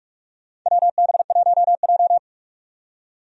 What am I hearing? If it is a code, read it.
Morse code, heard as a keyed tone, about 41 wpm, 700 Hz. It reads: WB1J
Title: USA Call Signs - 4 Characters - 45wpm